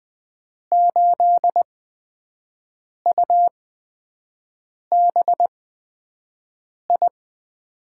Morse code, heard as a keyed tone, about 20 wpm, spaced as for 5 wpm, 700 Hz. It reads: 8UBI
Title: USA Call Signs - 6 Characters - 8x Character Spacing - 20wpm